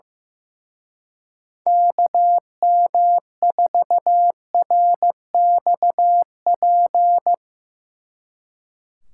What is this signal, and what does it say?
Morse code, heard as a keyed tone, about 16 wpm, 700 Hz. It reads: KM4RXP
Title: Speed Racing - USA 2x3 Call Signs 15wpm